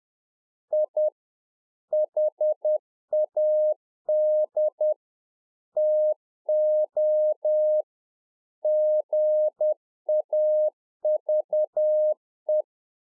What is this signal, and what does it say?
Morse code, heard as a keyed tone, about 10 wpm, 610 Hz. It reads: I HAD TO GAVE
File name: 2011-04-mp3 / A SMALL TRANCEIVER 1 10WPM